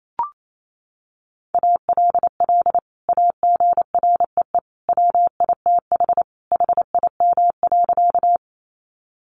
Morse code, heard as a keyed tone, about 28 wpm, 700 Hz. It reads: ALL AGREE WITH HIM.
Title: Sentences from Top 1100 Words 28wpm